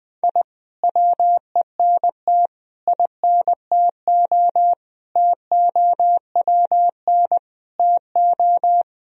Morse code, heard as a keyed tone, about 21 wpm, 700 Hz. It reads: I WENT INTO TOWN TO
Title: Sentences from Top 1000 Words - Review Part 4 - 20wpm